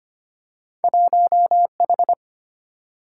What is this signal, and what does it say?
Morse code, heard as a keyed tone, about 25 wpm, 700 Hz. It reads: 1H
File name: Two Letter-Number Combinations - 25wpm